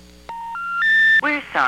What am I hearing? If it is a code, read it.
Special information tone, with speech.
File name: VerizonPrepaid